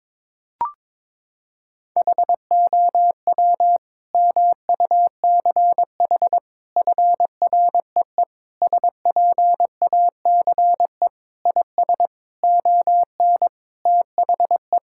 Morse code, heard as a keyed tone, about 21 wpm, 700 Hz. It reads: HOW MUCH FREE SPACE IS ON THE
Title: Sentences from Top 1000 Words - Review Part 6 - 22wpm